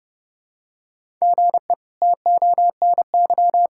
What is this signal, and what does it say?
Morse code, heard as a keyed tone, about 30 wpm, 700 Hz. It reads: GE TONY
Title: K1USN SST QSO Practice - 30wpm